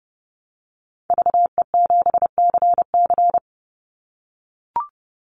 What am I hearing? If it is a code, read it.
Morse code, heard as a keyed tone, about 30 wpm, 700 Hz. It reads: VE7CC
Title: Call Signs - 30wpm